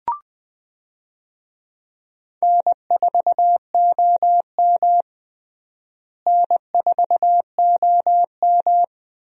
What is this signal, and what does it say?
Morse code, heard as a keyed tone, about 20 wpm, 700 Hz. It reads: N4OM N4OM
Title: USA Call Signs - 4 Characters - Repeated 2x - 20wpm